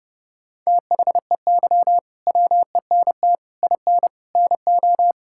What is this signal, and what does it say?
Morse code, heard as a keyed tone, about 30 wpm, 700 Hz. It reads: THEY WENT IN NO
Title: Sentences from Top 1000 Words - Review Part 5 - 30wpm